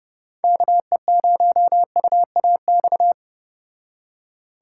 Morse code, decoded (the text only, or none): KE0UAX